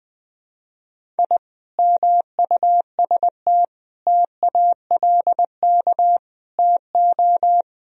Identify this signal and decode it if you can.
Morse code, heard as a keyed tone, about 21 wpm, 700 Hz. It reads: I MUST TALK TO
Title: Sentences from Top 1000 Words - Review Part 1 - 20wpm